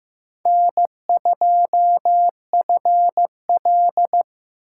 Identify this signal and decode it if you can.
Morse code, heard as a keyed tone, about 18 wpm, 700 Hz. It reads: N2FL
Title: Speed Racing - USA 1x2 Call Signs 15wpm